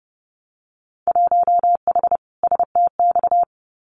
Morse code, heard as a keyed tone, about 30 wpm, 700 Hz. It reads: 1H STX
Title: Winter Field Day Exchanges Encore 30wpm